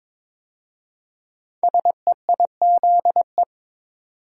Morse code, heard as a keyed tone, about 22 wpm, 700 Hz. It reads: SEIZE
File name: 5 Letter Words - First 3 Letters Given - 22wpm